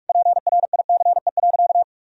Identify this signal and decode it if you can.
Morse code, heard as a keyed tone, about 46 wpm, 700 Hz. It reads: PRICE.
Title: Sentences from Top 1000 Words - Review Part 4 - 45wpm